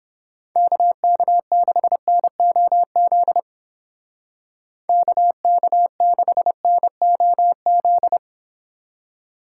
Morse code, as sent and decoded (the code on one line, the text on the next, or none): -.- -.- -.... -. --- --.. / -.- -.- -.... -. --- --..
KK6NOZ KK6NOZ